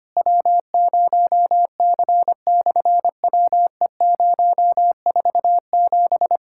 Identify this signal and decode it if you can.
Morse code, heard as a keyed tone, about 25 wpm, 700 Hz. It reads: W0C/WE047
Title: SOTA QSO Practice 25wpm